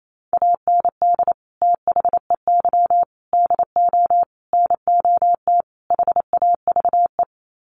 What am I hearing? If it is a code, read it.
Morse code, heard as a keyed tone, about 28 wpm, 700 Hz. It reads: AND THEY DO NOT HAVE
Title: Sets of 5 Words Encore2 28wpm